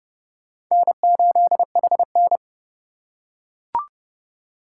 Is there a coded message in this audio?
Yes: Morse code from 0.7 to 2.4 s, heard as a keyed tone, about 30 wpm, 700 Hz. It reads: N8HN